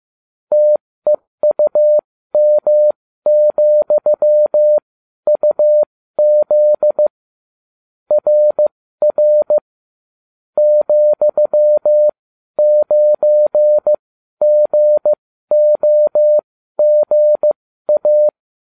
Morse code, heard as a keyed tone, about 15 wpm, spaced as for 12 wpm, 600 Hz. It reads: TEUM,UZ RR ,9GOGA